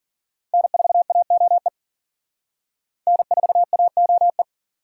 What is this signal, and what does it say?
Morse code, heard as a keyed tone, about 50 wpm, 700 Hz. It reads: N4AOE N4AOE